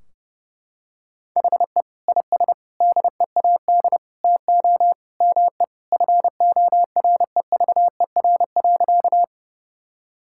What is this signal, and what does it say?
Morse code, heard as a keyed tone, about 30 wpm, 700 Hz. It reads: HE IS DEAD TO ME FOREVER.
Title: Sentences from Top 1600 Words 30wpm